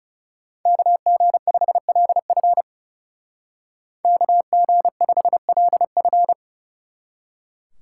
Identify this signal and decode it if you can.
Morse code, heard as a keyed tone, about 36 wpm, 700 Hz. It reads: KG5LF KG5LF
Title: Speed Racing - USA 2x2 Call Signs 30wpm